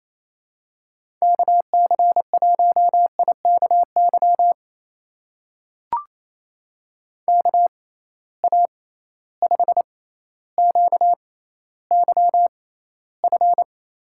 Morse code, heard as a keyed tone, about 28 wpm, spaced as for 14 wpm, 700 Hz. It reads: KC1IKY KA5QYF
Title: USA Call Signs - 6 Characters - 6x Character Spacing - 28wpm